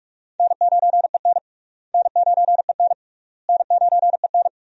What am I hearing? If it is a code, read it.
Morse code, heard as a keyed tone, about 46 wpm, 700 Hz. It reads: N9EN N9EN N9EN
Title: USA Call Signs - 4 Characters - Repeated 3x - 45wpm